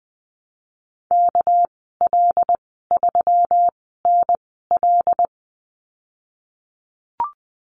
Morse code, heard as a keyed tone, about 19 wpm, spaced as for 14 wpm, 700 Hz. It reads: KL3NL
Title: 200 Lesson 32g - Letter k - Call Signs 20wpm